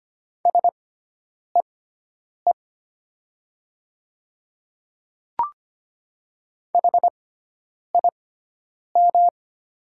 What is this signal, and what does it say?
Morse code, heard as a keyed tone, about 25 wpm, spaced as for 9 wpm, 700 Hz. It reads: SEE HIM
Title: 6x Character Spacing - Top 100 Words - 25wpm